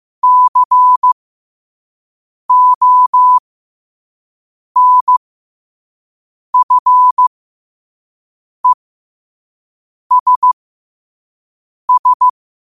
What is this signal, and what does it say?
Morse code, heard as a keyed tone, about 15 wpm, spaced as for 5 wpm, 1000 Hz. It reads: CONFESS
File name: qotdfiles / 5wpm